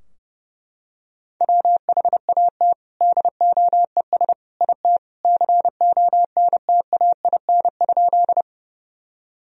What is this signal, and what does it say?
Morse code, heard as a keyed tone, about 32 wpm, 700 Hz. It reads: WHAT DOES IT CONTAIN?
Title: Sentences from Top 1000 Words - Review Part 2 - 30wpm